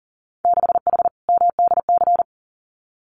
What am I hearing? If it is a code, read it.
Morse code, heard as a keyed tone, about 40 wpm, 700 Hz. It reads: 6H MDC
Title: Winter Field Day Exchanges Encore 40wpm